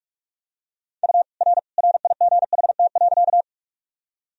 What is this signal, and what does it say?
Morse code, heard as a keyed tone, about 47 wpm, 700 Hz. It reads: U R RIGHT.
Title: Sentences from Top 200 Words and Abbreviations 45wpm